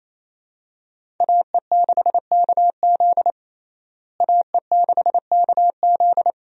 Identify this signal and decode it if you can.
Morse code, heard as a keyed tone, about 28 wpm, 700 Hz. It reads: AE6KZ AE6KZ
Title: USA Call Signs - 5 Characters - Repeated 2x - 28wpm